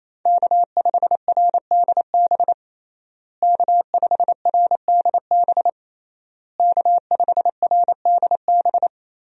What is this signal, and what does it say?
Morse code, heard as a keyed tone, about 28 wpm, 700 Hz. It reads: K5RDB K5RDB K5RDB
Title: USA Call Signs - 5 Characters - Repeated 3x - 28wpm